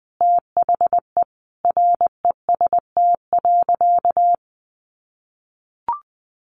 Morse code, heard as a keyed tone, about 20 wpm, 700 Hz. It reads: THE REST.
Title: Sentences from Top 1100 Words 20wpm